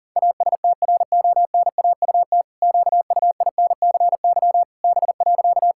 Morse code, heard as a keyed tone, about 37 wpm, 700 Hz. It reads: ASTRONAUT QUINCY B.
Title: Pangram Challenge v2 - 40wpm